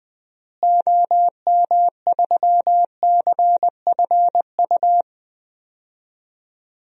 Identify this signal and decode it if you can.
Morse code, heard as a keyed tone, about 20 wpm, 700 Hz. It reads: OM3CFU
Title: DXCC-100 Call Signs - 6 Characters - Repeated 3x - 20wpm